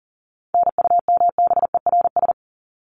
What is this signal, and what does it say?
Morse code, heard as a keyed tone, about 40 wpm, 700 Hz. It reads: NUMBERS
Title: ES:S Suffix - Top 100 Words with Base Words 40wpm